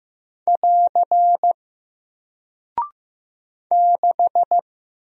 Morse code, heard as a keyed tone, about 15 wpm, 700 Hz. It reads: <AR> 6
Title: Single Character (Letter:Number:Punctuation:Prosign) Alt 15wpm